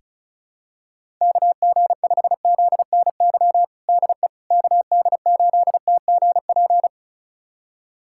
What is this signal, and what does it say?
Morse code, heard as a keyed tone, about 36 wpm, 700 Hz. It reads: KG5ZNY DE KD8TGP
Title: USA Callsign DE Callsign - 6 Characters - 35wpm